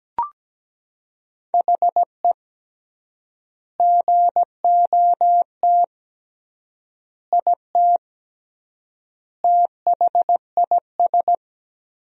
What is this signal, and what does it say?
Morse code, heard as a keyed tone, about 17 wpm, 700 Hz. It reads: HE GOT IT THIS
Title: 3x Word-Spacing - Sentences from Top 400 Words - 17wpm